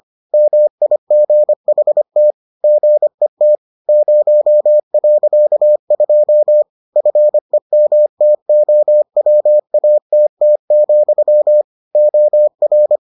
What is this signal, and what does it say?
Morse code, heard as a keyed tone, about 25 wpm, 600 Hz. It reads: MIGHT GET 0.2 FEMTOWATT, OR